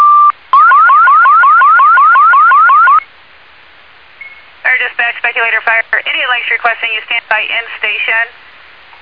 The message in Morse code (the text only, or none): none